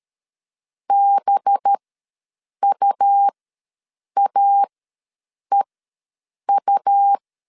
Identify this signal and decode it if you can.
Morse code, heard as a keyed tone, about 13 wpm, spaced as for 7 wpm, 800 Hz. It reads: BUAEU
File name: LEZIONI / 5 LBVC / 5 LBV ws 12wpm 3015